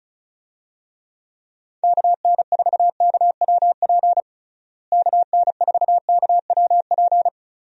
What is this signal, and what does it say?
Morse code, heard as a keyed tone, about 36 wpm, 700 Hz. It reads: KN4KWP KN4KWP
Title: USA Call Signs - 6 Characters - Repeated 2x - 35wpm